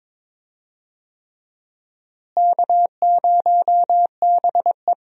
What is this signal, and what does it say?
Morse code, heard as a keyed tone, about 22 wpm, 700 Hz. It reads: K0BE
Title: USA Call Signs - 4 Characters - Repeated 3x - 22wpm